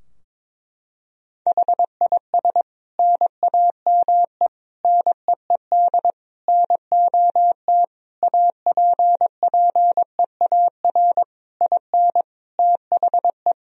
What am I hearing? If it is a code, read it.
Morse code, heard as a keyed tone, about 22 wpm, 700 Hz. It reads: HIS NAME NEED NOT APPEAR IN THE
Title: Sentences from Top 1000 Words - Review Part 2 - 22wpm